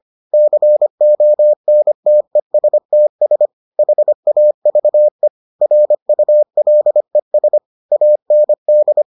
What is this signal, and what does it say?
Morse code, heard as a keyed tone, about 25 wpm, 600 Hz. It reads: CONTESTS HAVE RULES AND